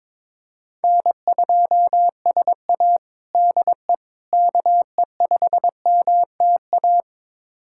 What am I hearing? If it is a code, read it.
Morse code, heard as a keyed tone, about 22 wpm, 700 Hz. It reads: N2SA DE KE5MTA
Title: USA Callsign DE Callsign 22wpm